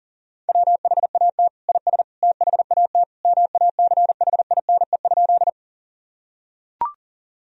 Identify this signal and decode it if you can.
Morse code, heard as a keyed tone, about 40 wpm, 700 Hz. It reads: WHAT IS THAT MACHINE?
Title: Sentences from Top 1000 Words - Review Part 2 - 40wpm